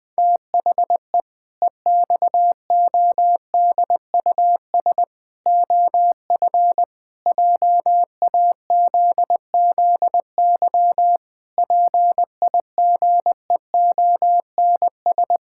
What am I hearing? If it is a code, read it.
Morse code, heard as a keyed tone, about 20 wpm, 700 Hz. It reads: THE EXODUS OF JAZZY PIGEONS
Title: Pangram Challenge v2 - 20wpm